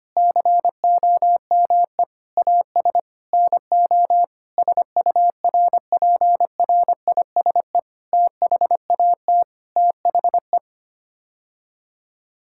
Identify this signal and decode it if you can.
Morse code, heard as a keyed tone, about 25 wpm, 700 Hz. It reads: COME AS NO SURPRISE THAT THE
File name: Sets of 9 Words - 25wpm